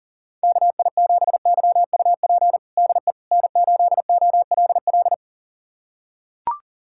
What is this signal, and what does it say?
Morse code, heard as a keyed tone, about 40 wpm, 700 Hz. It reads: KI7YUP DE N8OLL
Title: USA Callsign DE Callsign 40wpm